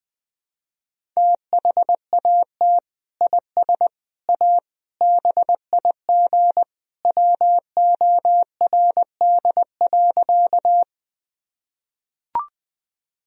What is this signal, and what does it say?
Morse code, heard as a keyed tone, about 20 wpm, 700 Hz. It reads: THAT IS A BIG WORD.
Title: Sentences from Top 1000 Words - Review Part 1 - 20wpm